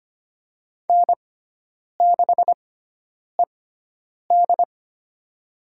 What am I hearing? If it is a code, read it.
Morse code, heard as a keyed tone, about 25 wpm, spaced as for 9 wpm, 700 Hz. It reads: N6ED
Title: USA Call Signs - 4 Characters - 6x Character Spacing - 25wpm